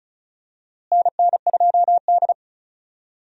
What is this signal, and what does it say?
Morse code, heard as a keyed tone, about 36 wpm, 700 Hz. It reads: NN2D
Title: USA Call Signs - 4 Characters - Repeated 3x - 35wpm